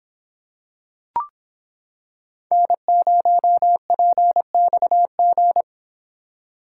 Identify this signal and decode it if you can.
Morse code, heard as a keyed tone, about 25 wpm, 700 Hz. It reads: N0PXG